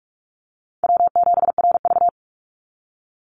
Morse code, heard as a keyed tone, about 44 wpm, 700 Hz. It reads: W7RV
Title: NA SOTA Chaser Callsigns 45wpm